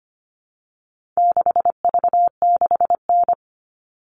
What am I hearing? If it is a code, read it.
Morse code, heard as a keyed tone, about 25 wpm, 700 Hz. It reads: 6V6N
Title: All Cut Numbers - 4-Digit 25wpm